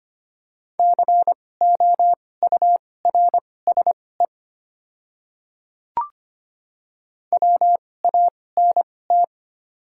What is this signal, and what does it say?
Morse code, heard as a keyed tone, about 25 wpm, spaced as for 18 wpm, 700 Hz. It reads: COURSE WANT
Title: 2x Character Spacing - Top 500 Words - 25wpm